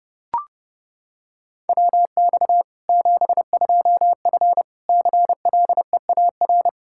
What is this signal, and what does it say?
Morse code, heard as a keyed tone, about 30 wpm, 700 Hz. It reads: WX 72F CLEAR
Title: Practice QSO - Exchange 2 - Info - 30wpm